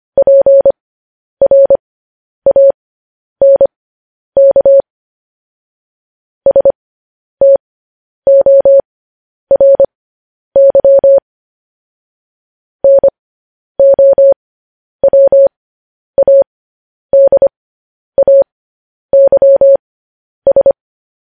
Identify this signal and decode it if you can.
Morse code, heard as a keyed tone, about 25 wpm, spaced as for 10 wpm, 550 Hz. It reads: PRANK STORY NOWADAYS
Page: News Headlines in Morse Code at 10 WPM